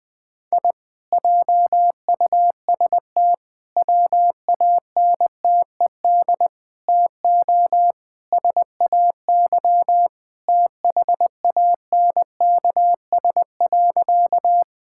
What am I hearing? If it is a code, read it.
Morse code, heard as a keyed tone, about 21 wpm, 700 Hz. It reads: I JUST WANTED TO SAY THANKS.
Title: Sentences with 4-Word Phrases - Spoken Beginning - 20wpm